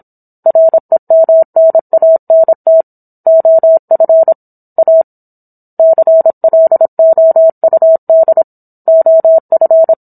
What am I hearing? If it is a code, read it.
Morse code, heard as a keyed tone, about 27 wpm, 650 Hz. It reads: REMNANT OF A CLOUD OF